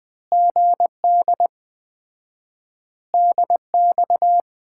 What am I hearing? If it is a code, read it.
Morse code, heard as a keyed tone, about 20 wpm, 700 Hz. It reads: GD DX
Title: Common QSO Phrases - 4x Word-Spacing - 20wpm